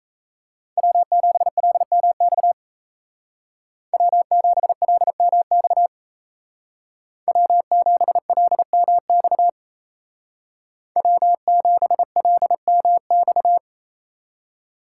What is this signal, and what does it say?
Morse code, heard as a keyed tone, about 36 wpm, 700 Hz. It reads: W7LMX W7LMX W7LMX W7LMX